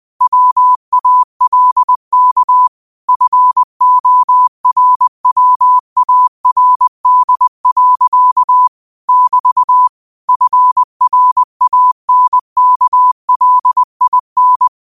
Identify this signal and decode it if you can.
Morse code, heard as a keyed tone, about 20 wpm, 1000 Hz. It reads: WALK FORWARD. <BT> FRANKLIN
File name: qotdfiles / 20wpm